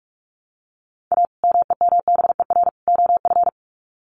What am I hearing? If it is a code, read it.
Morse code, heard as a keyed tone, about 44 wpm, 700 Hz. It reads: A MEMBER OF
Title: Sets of 3 Words 45wpm